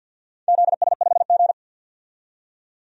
Morse code, heard as a keyed tone, about 53 wpm, 700 Hz. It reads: ZS5G
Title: DXCC-100 Call Signs - 4 Characters - 50wpm